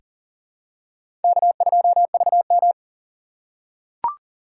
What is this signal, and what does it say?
Morse code, heard as a keyed tone, about 40 wpm, 700 Hz. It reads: K2VM
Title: USA Call Signs - 4 Characters - 2x Character Spacing - 40wpm